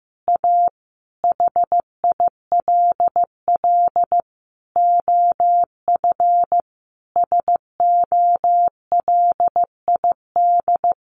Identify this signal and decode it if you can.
Morse code, heard as a keyed tone, about 15 wpm, 700 Hz. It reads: A HILL OF SOLID
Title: A Sound Of Thunder 15wpm